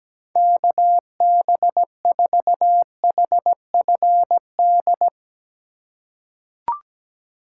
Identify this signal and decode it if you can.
Morse code, heard as a keyed tone, about 17 wpm, 700 Hz. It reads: KB4HFD
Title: USA Call Signs - 2x Character Spacing - 17wpm